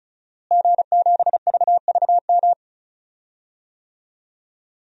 Morse code, heard as a keyed tone, about 36 wpm, 700 Hz. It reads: G7VVM
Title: DXCC-100 Call Signs - 5 Characters - Repeated 3x - 35wpm